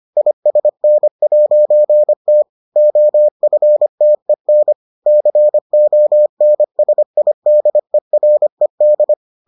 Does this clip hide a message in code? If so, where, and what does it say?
Yes: Morse code from 0.2 to 9.1 s, heard as a keyed tone, about 25 wpm, 600 Hz. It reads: ISN'T OFTEN CONSIDERED